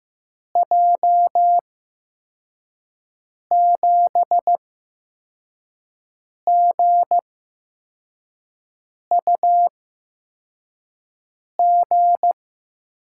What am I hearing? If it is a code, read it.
Morse code, heard as a keyed tone, about 15 wpm, spaced as for 4 wpm, 700 Hz. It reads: J7GUG
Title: USA Call Signs - 6 Characters - 8x Character Spacing - 15wpm